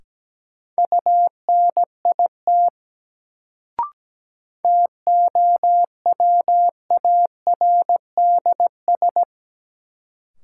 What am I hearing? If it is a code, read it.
Morse code, heard as a keyed tone, about 17 wpm, 700 Hz. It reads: UNIT TOWARDS
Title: Sentences from Top 1000 Words - Review Part 2 - New Word Drill - 17wpm